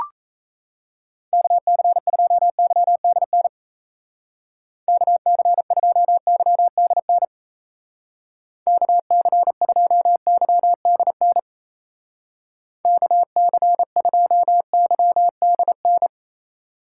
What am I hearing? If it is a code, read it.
Morse code, heard as a keyed tone, about 35 wpm, 700 Hz. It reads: KC2YDN KC2YDN KC2YDN KC2YDN